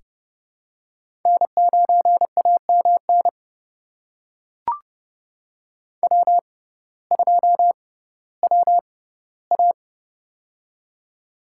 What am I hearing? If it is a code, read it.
Morse code, heard as a keyed tone, about 30 wpm, spaced as for 17 wpm, 700 Hz. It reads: N9AMN W2WA
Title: USA Call Signs - 6x Character Spacing - 30wpm